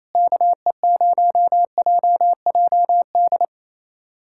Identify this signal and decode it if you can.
Morse code, heard as a keyed tone, about 28 wpm, 700 Hz. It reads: KE0JJD
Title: USA Callsign DE Callsign 28wpm